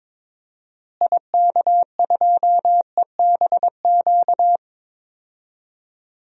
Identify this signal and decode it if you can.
Morse code, heard as a keyed tone, about 22 wpm, 700 Hz. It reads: IK2EBQ
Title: DXCC-100 Call Signs - 6 Characters - Repeated 3x - 22wpm